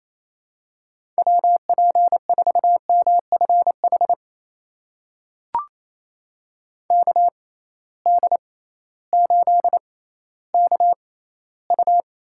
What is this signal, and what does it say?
Morse code, heard as a keyed tone, about 28 wpm, spaced as for 15 wpm, 700 Hz. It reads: WP4MFH KD8KU
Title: USA Call Signs - 6 Characters - 6x Character Spacing - 28wpm